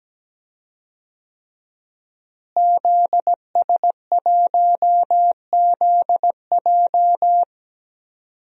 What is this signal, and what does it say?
Morse code, heard as a keyed tone, about 17 wpm, 700 Hz. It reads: ZS1ZJ
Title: DXCC-100 Call Signs - 5 Characters - Repeated 3x - 17wpm